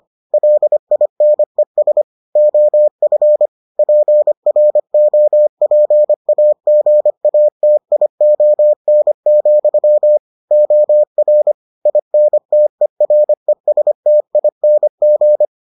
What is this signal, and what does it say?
Morse code, heard as a keyed tone, about 25 wpm, 600 Hz. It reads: LINES OF PROPAGATION, OR INTERESTING